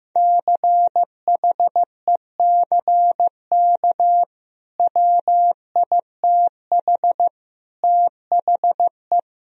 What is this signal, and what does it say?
Morse code, heard as a keyed tone, about 15 wpm, 700 Hz. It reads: CHECK WITH THE
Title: Sentences from Top 1000 Words - Review Part 3 - 15wpm